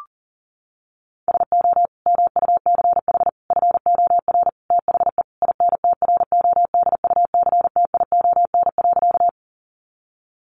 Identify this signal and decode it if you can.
Morse code, heard as a keyed tone, about 40 wpm, 700 Hz. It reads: SO MUCH FOR THE INTRODUCTION.
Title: Sentences from Top 1000 Words 40wpm